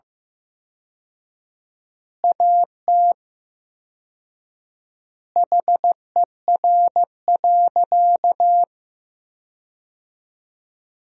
Morse code, heard as a keyed tone, about 15 wpm, 700 Hz. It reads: AT HER.
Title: 4x Word-Spacing - Sentences from Top 200 Words - 15wpm